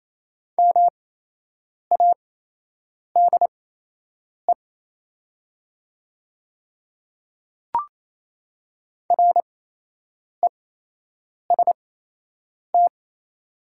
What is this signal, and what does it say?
Morse code, heard as a keyed tone, about 28 wpm, spaced as for 8 wpm, 700 Hz. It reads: MADE REST
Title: 8x Character Spacing - Top 500 Words - 28wpm